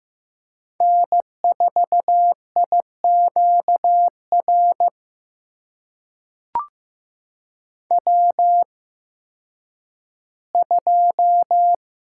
Morse code, heard as a keyed tone, about 15 wpm, spaced as for 10 wpm, 700 Hz. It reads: N4IQR W2
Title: USA Call Signs - 5 Characters - 8x Character Spacing - 15wpm